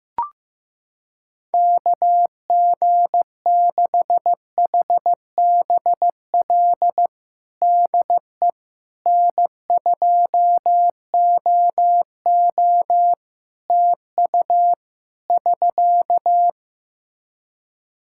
Morse code, heard as a keyed tone, about 16 wpm, 700 Hz. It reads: KG6HBL DE N2OO TU <SK>
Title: K1USN SST QSO Practice 15wpm